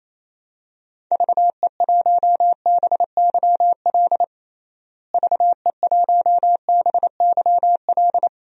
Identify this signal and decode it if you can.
Morse code, heard as a keyed tone, about 27 wpm, 700 Hz. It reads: VE1BYL VE1BYL
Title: DXCC-100 Call Signs - 6 Characters - Repeated 3x - 28wpm